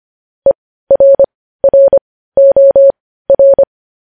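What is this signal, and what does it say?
Morse code, heard as a keyed tone, about 25 wpm, spaced as for 15 wpm, 550 Hz. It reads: ERROR
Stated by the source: Morse code news headlines at 15 WPM.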